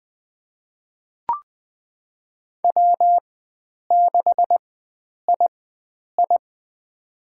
Morse code, heard as a keyed tone, about 20 wpm, spaced as for 9 wpm, 700 Hz. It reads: W6II
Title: USA Call Signs - 4 Characters - 4x Character Spacing - 20wpm